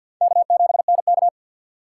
Morse code, heard as a keyed tone, about 51 wpm, 700 Hz. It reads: K7NK
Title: USA Call Signs - 4 Characters - 6x Character Spacing - 50wpm